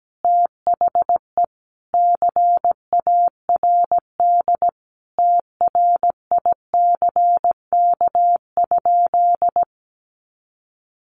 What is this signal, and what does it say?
Morse code, heard as a keyed tone, about 17 wpm, 700 Hz. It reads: THE CARD TRICK?
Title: Sentences from Top 1500 Words 17wpm